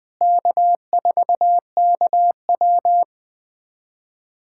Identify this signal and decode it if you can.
Morse code, heard as a keyed tone, about 20 wpm, 700 Hz. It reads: K4KW
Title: USA Callsign DE Callsign - 4 Characters - 20wpm